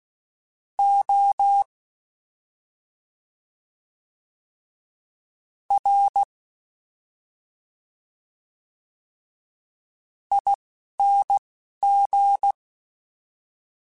Morse code, heard as a keyed tone, about 16 wpm, spaced as for 4 wpm, 800 Hz. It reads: ORING
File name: Special INGs-mp3s / Special INGs-mp3s / Slow ing @10